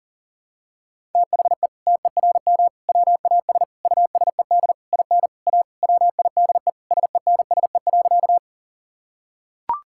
Morse code, heard as a keyed tone, about 40 wpm, 700 Hz. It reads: THE TERM WAS USED IN A WIDE SENSE.
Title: Sentences from Top 1000 Words - Review Part 6 - 40wpm